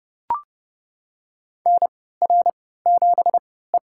Morse code, heard as a keyed tone, about 30 wpm, spaced as for 17 wpm, 700 Hz. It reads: NR7E
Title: USA Call Signs - 4 Characters - 3x Character Spacing - 30wpm